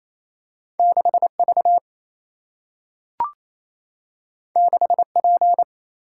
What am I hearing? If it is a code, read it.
Morse code, heard as a keyed tone, about 28 wpm, 700 Hz. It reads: 6V 6P
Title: Single-Letter Single-Number 28wpm